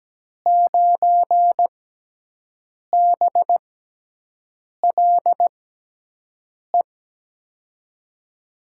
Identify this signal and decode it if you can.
Morse code, heard as a keyed tone, about 17 wpm, spaced as for 6 wpm, 700 Hz. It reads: 9BLE
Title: USA Call Signs - 6 Characters - 6x Character Spacing - 17wpm